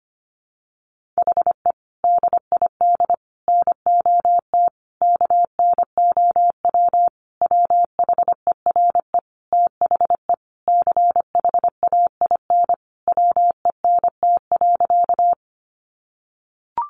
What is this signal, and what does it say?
Morse code, heard as a keyed tone, about 25 wpm, 700 Hz. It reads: HE DID NOT KNOW WHERE THE CHAIN WENT.
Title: Sentences from Top 1500 Words 25wpm